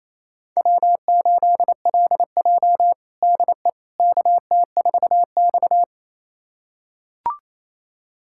Morse code, heard as a keyed tone, about 28 wpm, 700 Hz. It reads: W8LJ DE KT4X
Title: USA Callsign DE Callsign - 4 Characters - 28wpm